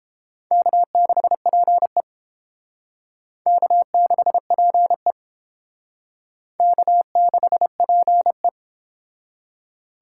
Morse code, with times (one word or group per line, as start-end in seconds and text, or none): 0.5-2.0 K6PE
3.5-5.1 K6PE
6.6-8.5 K6PE